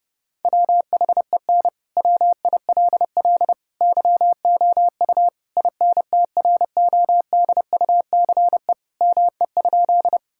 Morse code, heard as a keyed tone, about 32 wpm, 700 Hz. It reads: WHEN WILL YOU INTRODUCE ME?
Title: Sentences from Top 1600 Words 30wpm